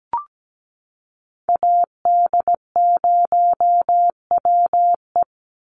Morse code, heard as a keyed tone, about 17 wpm, 700 Hz. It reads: AD0WE
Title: Call Signs Repeated 2x - 17wpm